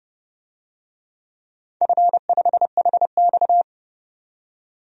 Morse code, heard as a keyed tone, about 30 wpm, 700 Hz. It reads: F5HX
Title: DXCC-100 Call Signs - 4 Characters - 30wpm